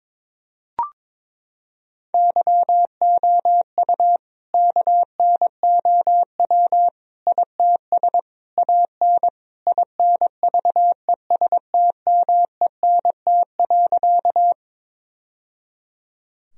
Morse code, heard as a keyed tone, about 22 wpm, 700 Hz. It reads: YOU KNOW ITS AN INVESTMENT.
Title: Sentences from Top 1000 Words - Review Part 4 - 22wpm